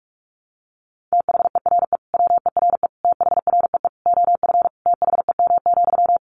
Morse code, heard as a keyed tone, about 44 wpm, 700 Hz. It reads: THERE WERE THREE OF THEM,
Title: Sentences from Top 800 Words 45wpm